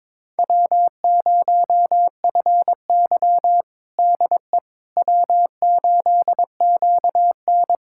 Morse code, heard as a keyed tone, about 22 wpm, 700 Hz. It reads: W0FY DE W8QN
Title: USA Callsign DE Callsign - 4 Characters - 22wpm